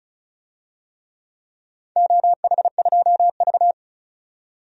Morse code, heard as a keyed tone, about 36 wpm, 700 Hz. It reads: OH2V
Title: DXCC-100 Call Signs - 4 Characters - 35wpm